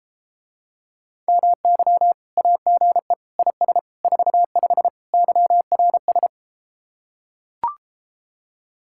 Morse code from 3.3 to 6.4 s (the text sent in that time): IS 45 YRS